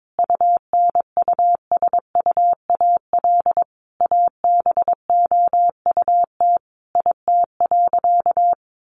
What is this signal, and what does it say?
Morse code, heard as a keyed tone, about 22 wpm, 700 Hz. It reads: UNUSUAL ABOUT IT.
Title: Sentences from Top 1400 Words 22wpm